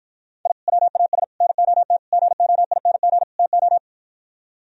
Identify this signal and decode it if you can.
Morse code, heard as a keyed tone, about 52 wpm, 700 Hz. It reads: I WAS NOT GOING TO